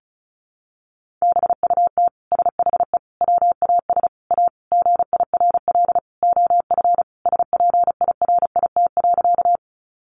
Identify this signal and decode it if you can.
Morse code, heard as a keyed tone, about 34 wpm, 700 Hz. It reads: BUT SHE WAS A GIRL OF SPIRIT.
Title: Sentences from Top 1400 Words 35wpm